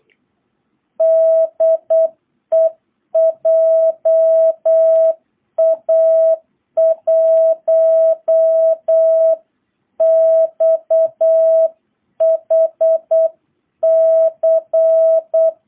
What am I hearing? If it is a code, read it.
Morse code, heard as a keyed tone, about 8 wpm, 660 Hz. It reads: DE JA1XHC